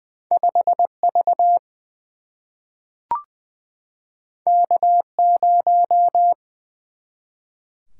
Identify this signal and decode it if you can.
Morse code, heard as a keyed tone, about 21 wpm, 700 Hz. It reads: 5 V K 0
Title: Single-Letter Single-Number 20wpm